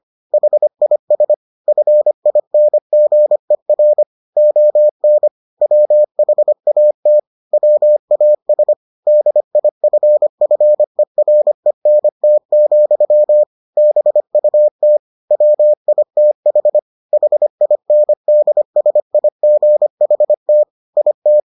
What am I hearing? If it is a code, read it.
Morse code, heard as a keyed tone, about 25 wpm, 600 Hz. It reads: HIS FINGER ON WHAT WAS DIFFERENT, BUT WITH HINDSIGHT IT